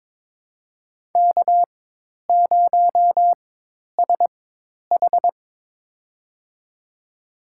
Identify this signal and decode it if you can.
Morse code, heard as a keyed tone, about 22 wpm, spaced as for 10 wpm, 700 Hz. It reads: K0SH
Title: USA Call Signs - 4 Characters - 4x Character Spacing - 22wpm